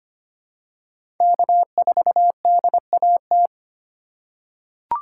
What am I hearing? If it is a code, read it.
Morse code, heard as a keyed tone, about 25 wpm, 700 Hz. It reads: K4DAT
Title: USA Call Signs - 4x Character Spacing - 25wpm